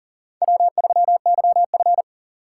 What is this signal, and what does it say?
Morse code, heard as a keyed tone, about 40 wpm, 700 Hz. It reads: W3YF
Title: USA Call Signs - 4x Character Spacing - 40wpm